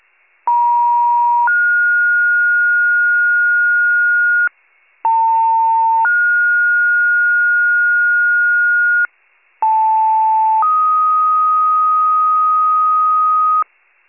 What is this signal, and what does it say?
Two-tone page, heard as a one-second tone, then a three-second tone, about 950 then 1430 Hz, about 900 then 1430 Hz, about 880 then 1250 Hz, with noise. It is heard from 0.5 to 13.6 s.